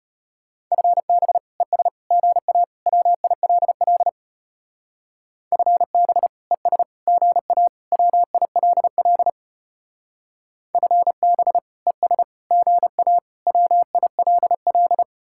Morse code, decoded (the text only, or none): FB ES GA WILL FB ES GA WILL FB ES GA WILL